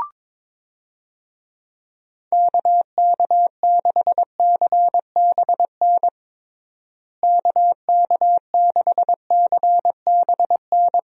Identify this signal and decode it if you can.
Morse code, heard as a keyed tone, about 22 wpm, 700 Hz. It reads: KK6CBN KK6CBN
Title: USA Call Signs - 6 Characters - Repeated 2x - 22wpm